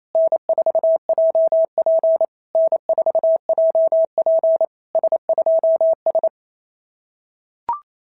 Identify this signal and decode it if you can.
Morse code, heard as a keyed tone, about 28 wpm, 650 Hz. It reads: N4JP N4JP S2S
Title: SOTA QSO Practice 28wpm